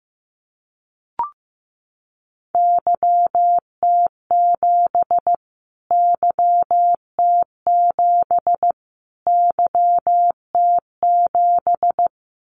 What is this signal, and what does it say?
Morse code, heard as a keyed tone, about 15 wpm, 700 Hz. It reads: YT7 YT7 YT7
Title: Callsign Prefixes for Top 25 DXCC - Repeated 3x - 15wpm